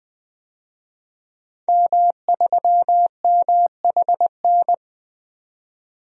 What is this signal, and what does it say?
Morse code, heard as a keyed tone, about 21 wpm, 700 Hz. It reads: M3MHN
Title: DXCC-100 Call Signs - 5 Characters - 20wpm